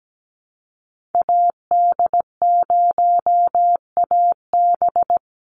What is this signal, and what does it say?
Morse code, heard as a keyed tone, about 17 wpm, 700 Hz. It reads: AD0AB
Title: Call Signs - 17wpm